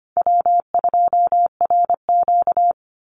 Morse code, heard as a keyed tone, about 25 wpm, 700 Hz. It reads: W2RQ
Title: Easy Call Signs - 25wpm